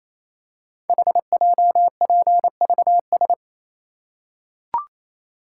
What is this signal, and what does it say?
Morse code, heard as a keyed tone, about 28 wpm, 700 Hz. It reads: HJPVS
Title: Code Groups - 5 Letters - 28wpm